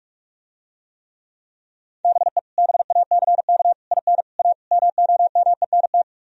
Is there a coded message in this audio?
Yes: Morse code from 2.0 to 6.0 s, heard as a keyed tone, about 45 wpm, 700 Hz. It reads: BE BACK IN A MOMENT